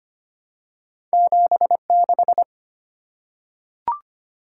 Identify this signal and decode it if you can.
Morse code, heard as a keyed tone, about 25 wpm, 700 Hz. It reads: 76
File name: Two Letter-Number Combinations - 25wpm